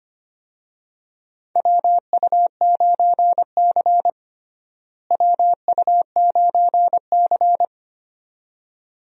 Morse code, heard as a keyed tone, about 25 wpm, 700 Hz. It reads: WU9C WU9C
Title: USA Call Signs - 4 Characters - Repeated 2x - 25wpm